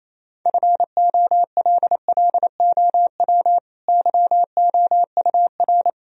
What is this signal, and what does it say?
Morse code, heard as a keyed tone, about 28 wpm, 700 Hz. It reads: FOLLOW YOUR
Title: Sentences from Top 1000 Words - Review Part 3 - 28wpm